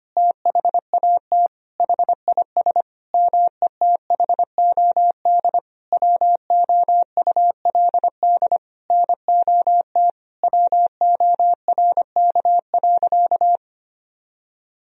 Morse code, heard as a keyed tone, about 25 wpm, 700 Hz. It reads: THAT HIS METHOD WOULD NOT WORK.
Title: Sentences from Top 1000 Words - Review Part 3 - 25wpm